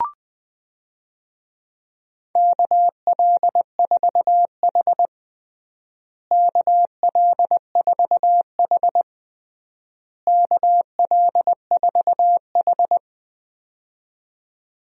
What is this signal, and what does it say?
Morse code, heard as a keyed tone, about 20 wpm, 700 Hz. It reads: KL4H KL4H KL4H
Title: USA Call Signs - 4 Characters - Repeated 3x - 20wpm